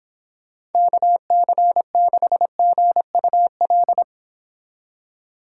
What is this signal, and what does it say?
Morse code, heard as a keyed tone, about 30 wpm, 700 Hz. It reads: KC6GUL